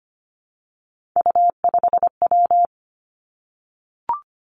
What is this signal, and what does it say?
Morse code, heard as a keyed tone, about 25 wpm, 700 Hz. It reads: U5W
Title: Three Challenging Letter-Number Combinations 25wpm